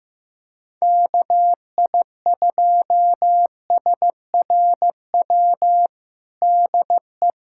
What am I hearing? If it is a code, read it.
Morse code, heard as a keyed tone, about 15 wpm, 700 Hz. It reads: KI2SRW DE
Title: USA Callsign DE Callsign 15wpm